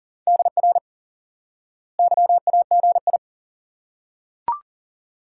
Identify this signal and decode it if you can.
Morse code, heard as a keyed tone, about 40 wpm, 700 Hz. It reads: DR YAGI
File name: IQE Course 3 - Lesson 89 - YAGI - Two Elements - 40wpm